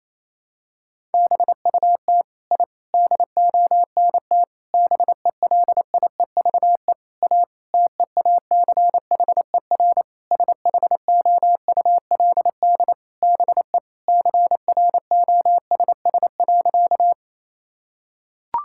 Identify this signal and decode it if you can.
Morse code, heard as a keyed tone, about 28 wpm, 700 Hz. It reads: BUT I DONT BELIEVE A TEACHER SHOULD BE CROSS.
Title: Sentences from Top 1000 Words - Review Part 6 - 28wpm